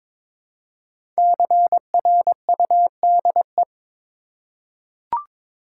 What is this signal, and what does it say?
Morse code, heard as a keyed tone, about 22 wpm, 700 Hz. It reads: CRUDE
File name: 5 Letter Words - First 3 Letters Given - 22wpm